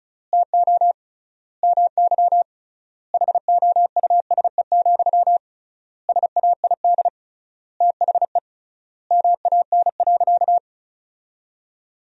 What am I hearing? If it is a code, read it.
Morse code, heard as a keyed tone, about 36 wpm, 700 Hz. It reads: TO MY HOUSE, SAID THE MAN.
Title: 3x Word-Spacing - Sentences from Top 400 Words - 35wpm